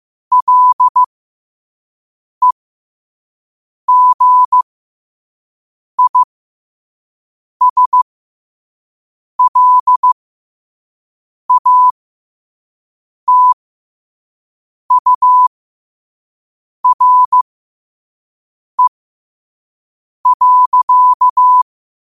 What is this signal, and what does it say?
Morse code, heard as a keyed tone, about 15 wpm, spaced as for 5 wpm, 1000 Hz. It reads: LEGISLATURE.
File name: qotdfiles / 5wpm